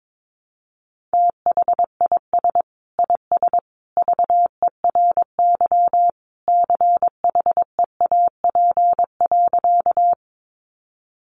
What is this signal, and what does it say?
Morse code, heard as a keyed tone, about 22 wpm, 700 Hz. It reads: THIS IS VERY CHEAP.
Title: Sentences from Top 1200 Words 22wpm